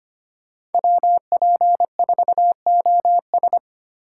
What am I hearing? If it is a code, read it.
Morse code, heard as a keyed tone, about 25 wpm, 700 Hz. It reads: WP4OS
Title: USA Call Signs - 5 Characters - 2x Character Spacing - 25wpm